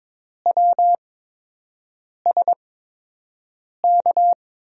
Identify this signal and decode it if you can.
Morse code, heard as a keyed tone, about 23 wpm, spaced as for 6 wpm, 700 Hz. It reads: WSK
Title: USA Call Signs - 5 Characters - 8x Character Spacing - 22wpm